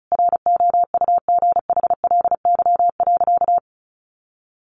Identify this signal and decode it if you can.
Morse code, heard as a keyed tone, about 34 wpm, 700 Hz. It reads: ROUGHLY.
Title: Sentences from Top 1500 Words 35wpm